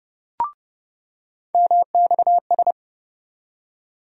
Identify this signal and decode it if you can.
Morse code, heard as a keyed tone, about 30 wpm, 700 Hz. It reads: MXS
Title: USA Call Sign Postfixes - 3 Letters 30wpm